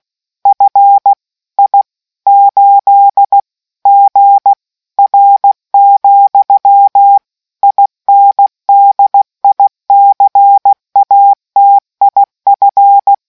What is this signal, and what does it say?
Morse code, heard as a keyed tone, about 16 wpm, 800 Hz. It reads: FI8GR, INDICATIF